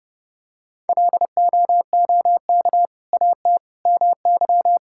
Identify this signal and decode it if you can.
Morse code, heard as a keyed tone, about 30 wpm, 700 Hz. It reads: LOOK AT MY
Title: Sentences from Top 1600 Words 30wpm